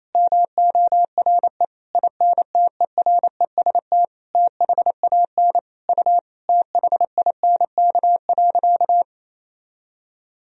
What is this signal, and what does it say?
Morse code, heard as a keyed tone, about 28 wpm, 700 Hz. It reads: MORE INTEREST THAN U THINK.
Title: Sentences from Top 400 Words and Abbreviations 28wpm